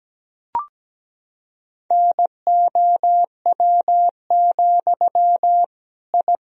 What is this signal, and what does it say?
Morse code, heard as a keyed tone, about 17 wpm, 700 Hz. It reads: NOW, I
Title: Sentences from Top 1000 Words - Review Part 3 - 17wpm